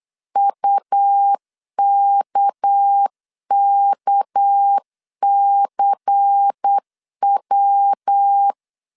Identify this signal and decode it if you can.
Morse code, heard as a keyed tone, about 8 wpm, 800 Hz. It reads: UKKCW
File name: LEZIONI / 6 PKX0 / 6 PK 8wpm 3003